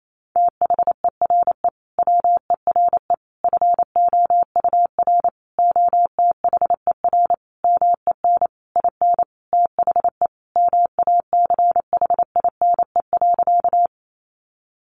Morse code, heard as a keyed tone, about 28 wpm, 700 Hz. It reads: THERE WERE FOUR OTHER MEN IN THE MACHINE.
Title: A Sound Of Thunder 28wpm